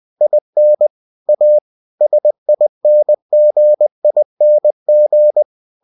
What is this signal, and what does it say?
Morse code, heard as a keyed tone, about 20 wpm, 600 Hz. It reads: IN A SINGING
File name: cw-books / winnie-the-pooh / winnie-20wpm-page-0039